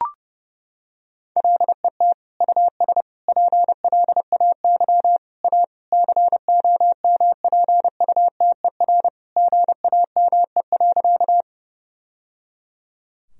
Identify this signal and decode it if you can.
Morse code, heard as a keyed tone, about 32 wpm, 700 Hz. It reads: LET US PLAY A COMPUTER GAME.
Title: Sentences from Top 1000 Words - Review Part 3 - 30wpm